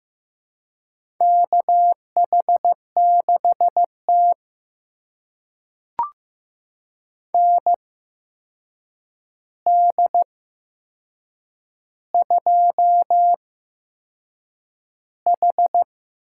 Morse code, heard as a keyed tone, about 15 wpm, spaced as for 6 wpm, 700 Hz. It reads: KH6T ND2H
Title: USA Call Signs - 4 Characters - 8x Character Spacing - 15wpm